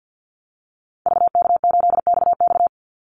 Morse code, heard as a keyed tone, about 51 wpm, 700 Hz. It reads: VK8XX
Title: Call Signs - 50wpm